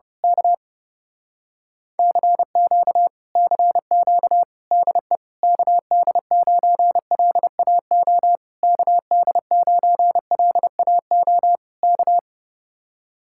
Morse code, heard as a keyed tone, about 29 wpm, 700 Hz. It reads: K CQ CQ DE KD9LAO KD9LAO K
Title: Speed-Racing Common QSO Phrases Encore - 30wpm